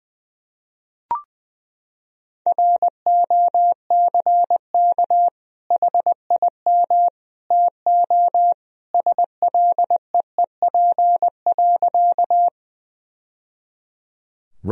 Morse code, heard as a keyed tone, about 21 wpm, 700 Hz. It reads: ROCK HIM TO SLEEP.
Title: Sentences from Top 1000 Words - Review Part 2 - 20wpm